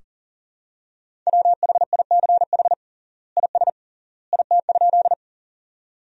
Morse code, heard as a keyed tone, about 40 wpm, 700 Hz. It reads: WHICH IS IT?
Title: 3x Word-Spacing - Sentences from Top 100 Words - 40wpm